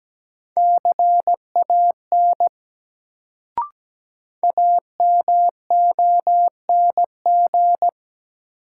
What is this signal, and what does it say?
Morse code, heard as a keyed tone, about 17 wpm, 700 Hz. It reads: CAN AMONG
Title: Sets of 2 Words - New Word Drill - 17wpm